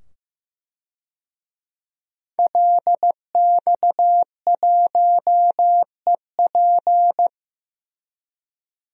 Morse code, heard as a keyed tone, about 15 wpm, 700 Hz. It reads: LX1EP
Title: DXCC-100 Call Signs - 5 Characters - 15wpm